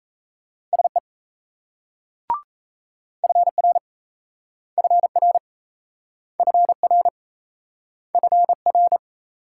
Morse code, heard as a keyed tone, about 35 wpm, 700 Hz. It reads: SE FR FR FR FR